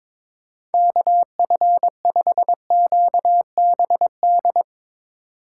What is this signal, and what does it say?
Morse code, heard as a keyed tone, about 22 wpm, 700 Hz. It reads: KF5QBD
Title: USA Call Signs - 6 Characters - 4x Character Spacing - 22wpm